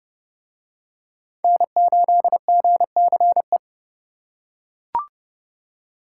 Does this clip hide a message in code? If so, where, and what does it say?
Yes: Morse code from 1.4 to 3.6 s, heard as a keyed tone, about 28 wpm, 700 Hz. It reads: N8GCE